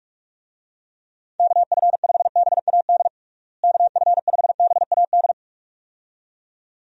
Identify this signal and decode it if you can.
Morse code, heard as a keyed tone, about 46 wpm, 700 Hz. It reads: KF5BAD KF5BAD
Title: USA Call Signs - 6 Characters - Repeated 2x - 45wpm